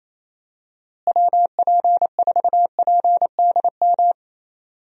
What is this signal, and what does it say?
Morse code, heard as a keyed tone, about 28 wpm, 700 Hz. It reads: WP4PDM
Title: USA Call Signs - 6 Characters - Repeated 3x - 28wpm